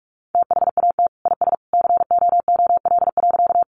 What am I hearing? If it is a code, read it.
Morse code, heard as a keyed tone, about 44 wpm, 700 Hz. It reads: THAT IS COOL.
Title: Sentences from Top 1200 Words 45wpm